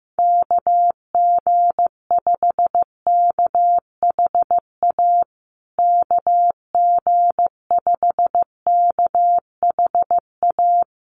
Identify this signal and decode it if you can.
Morse code, heard as a keyed tone, about 15 wpm, 700 Hz. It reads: KG5KHA KG5KHA
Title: USA Call Signs Repeated 3x 15wpm